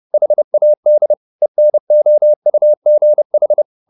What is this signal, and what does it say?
Morse code, heard as a keyed tone, about 30 wpm, 600 Hz. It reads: HAD ENOUGH